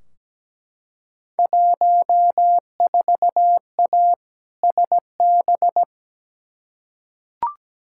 Morse code, heard as a keyed tone, about 17 wpm, 700 Hz. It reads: 14A SB
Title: Field Day Exchanges Encore - 17wpm